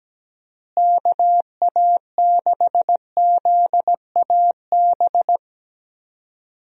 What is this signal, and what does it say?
Morse code, heard as a keyed tone, about 17 wpm, 700 Hz. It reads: KA6ZAB
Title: USA Call Signs - 6 Characters - Repeated 2x - 17wpm